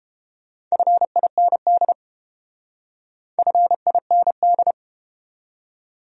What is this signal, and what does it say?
Morse code, heard as a keyed tone, about 31 wpm, 700 Hz. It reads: FIND FIND